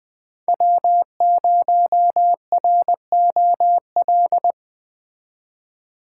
Morse code, heard as a keyed tone, about 21 wpm, 700 Hz. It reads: W0ROL
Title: Speed Racing - USA 1x3 Call Signs 20wpm